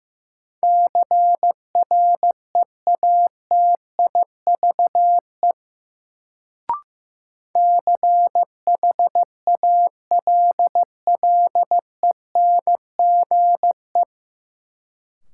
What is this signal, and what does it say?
Morse code, heard as a keyed tone, about 15 wpm, 700 Hz. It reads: CREATIVE CHALLENGE
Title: Sentences from Top 1300 Words - New Word Drill Alt - 15wpm